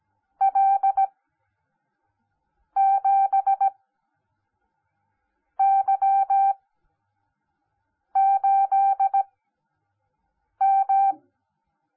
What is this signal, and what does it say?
Morse code, heard as a keyed tone, about 17 wpm, spaced as for 5 wpm, 780 Hz. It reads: L7Y8M
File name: Onlinelehrgang / morsen / DJ4UF Morsen10